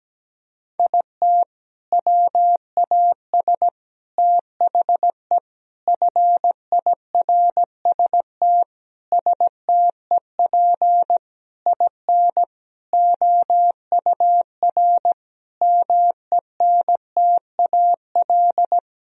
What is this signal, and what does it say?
Morse code, heard as a keyed tone, about 17 wpm, 700 Hz. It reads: IT WAS THE FIRST STEP IN OUR MENTAL
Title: Sentences from Top 1000 Words - Review Part 4 - 17wpm